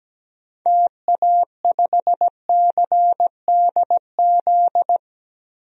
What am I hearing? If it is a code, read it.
Morse code, heard as a keyed tone, about 17 wpm, 700 Hz. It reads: TA5CDZ
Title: DXCC-100 Call Signs - 6 Characters - Repeated 2x - 17wpm